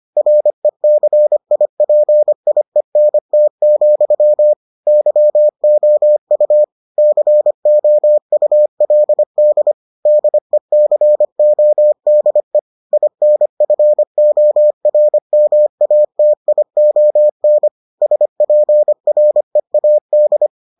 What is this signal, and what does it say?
Morse code, heard as a keyed tone, about 25 wpm, 600 Hz. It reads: RECIPIENT, YOU COULD DECODE INFORMATION SPREAD